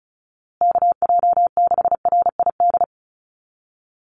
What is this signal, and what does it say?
Morse code, heard as a keyed tone, about 34 wpm, 700 Hz. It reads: KJ6RID
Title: USA Call Signs Repeated 3x 35wpm